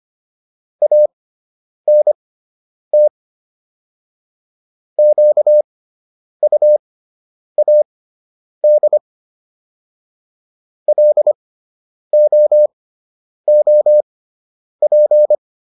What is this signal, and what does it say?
Morse code, heard as a keyed tone, about 25 wpm, spaced as for 9 wpm, 600 Hz. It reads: ANT QUAD LOOP